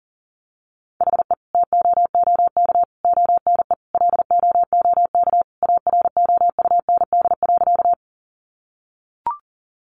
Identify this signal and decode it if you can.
Morse code, heard as a keyed tone, about 37 wpm, 700 Hz. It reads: HE TOOK ONE LOOK AROUND.
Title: Sentences from Top 600 Words 40wpm